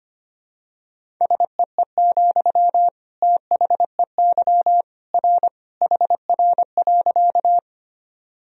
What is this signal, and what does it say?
Morse code, heard as a keyed tone, about 25 wpm, 700 Hz. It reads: SEE, THEY R HR.
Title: Sentences from Top 200 Words and Abbreviations 25wpm